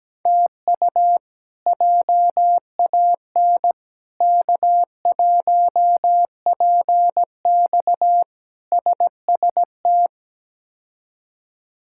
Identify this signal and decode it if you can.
Morse code, heard as a keyed tone, about 17 wpm, 700 Hz. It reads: TU JAN K1PX SST
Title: K1USN SST QSO Practice - 17wpm